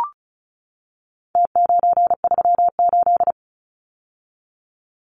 Morse code, heard as a keyed tone, about 34 wpm, 700 Hz. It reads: T938
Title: Numbers - Contest Style - 4-Digit 35wpm